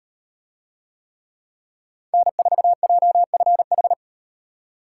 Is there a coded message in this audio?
Yes: Morse code from 2.1 to 3.9 s, heard as a keyed tone, about 36 wpm, 700 Hz. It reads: N4JFH